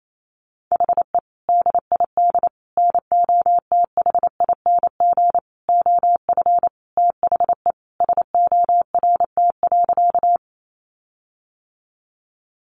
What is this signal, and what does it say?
Morse code, heard as a keyed tone, about 28 wpm, 700 Hz. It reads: HE DID NOTHING OF THE SORT.
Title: Sentences from Top 1300 Words 28wpm